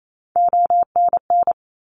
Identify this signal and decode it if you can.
Morse code, heard as a keyed tone, about 28 wpm, 700 Hz. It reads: ONN
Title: ARRL Sections 28wpm